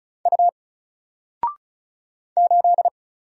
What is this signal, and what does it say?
Morse code, heard as a keyed tone, about 36 wpm, 700 Hz. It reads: U 8
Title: Single Character (Letter:Number:Punctuation:Prosign) 35wpm